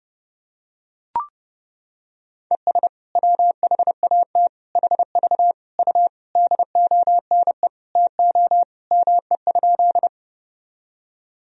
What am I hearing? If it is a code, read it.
Morse code, heard as a keyed tone, about 32 wpm, 700 Hz. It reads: ES WHAT HV U DONE TO ME?
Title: Sentences from Top 500 Words and Abbreviations 30wpm